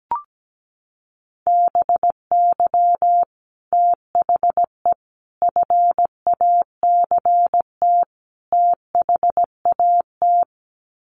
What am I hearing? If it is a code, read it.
Morse code, heard as a keyed tone, about 17 wpm, 700 Hz. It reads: BY THE FACT THAT
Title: Sets of 4 Words 17wpm